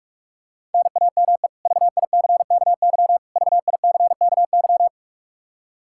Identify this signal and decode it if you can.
Morse code, heard as a keyed tone, about 46 wpm, 700 Hz. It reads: NAME VICKY VICKY
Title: K1USN SST QSO Practice 45wpm